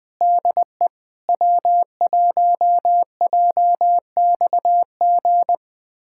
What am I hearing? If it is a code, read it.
Morse code, heard as a keyed tone, about 21 wpm, 700 Hz. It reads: DE W1JXG
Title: USA Callsign DE Callsign 20wpm